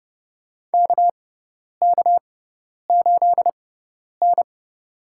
Morse code, heard as a keyed tone, about 31 wpm, spaced as for 10 wpm, 700 Hz. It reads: KK8N
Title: USA Call Signs - 4 Characters - 6x Character Spacing - 30wpm